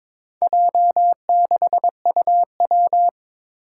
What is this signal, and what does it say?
Morse code, heard as a keyed tone, about 21 wpm, 700 Hz. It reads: J6UW
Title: Four Challenging Letter-Number Combinations 22wpm